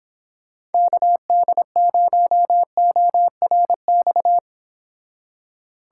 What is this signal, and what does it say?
Morse code, heard as a keyed tone, about 29 wpm, 700 Hz. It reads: KD0ORX